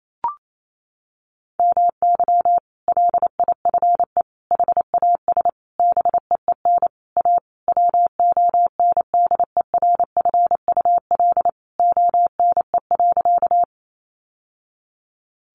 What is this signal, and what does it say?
Morse code, heard as a keyed tone, about 28 wpm, 700 Hz. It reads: MY LIFE HAS BEEN A WONDERFUL ONE.
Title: Sentences from Top 1100 Words 28wpm